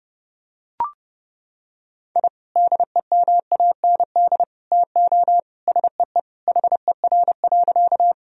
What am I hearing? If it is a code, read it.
Morse code, heard as a keyed tone, about 30 wpm, 700 Hz. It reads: I DEMAND TO SEE HER.
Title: Sentences from Top 1000 Words - Review Part 4 - 30wpm